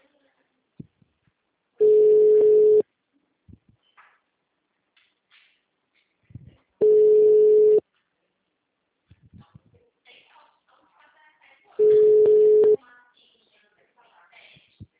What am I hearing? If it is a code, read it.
Ringback tone (Europe, 425 Hz), with speech.
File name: out- -2021- - - .171